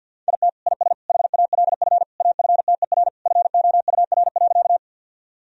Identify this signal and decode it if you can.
Morse code, heard as a keyed tone, about 52 wpm, 700 Hz. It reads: IT IS HALF AFTER FOUR.
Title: Sentences from Top 1000 Words - Review Part 1 - 50wpm